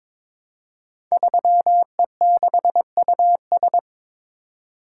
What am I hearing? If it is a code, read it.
Morse code, heard as a keyed tone, about 22 wpm, 700 Hz. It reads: 3E6US
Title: Code Groups - 5 Letter:Numbers - 22wpm